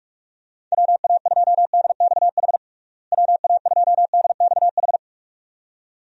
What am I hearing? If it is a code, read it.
Morse code, heard as a keyed tone, about 46 wpm, 700 Hz. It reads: WA2DXH WA2DXH
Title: USA Call Signs - 6 Characters - Repeated 2x - 45wpm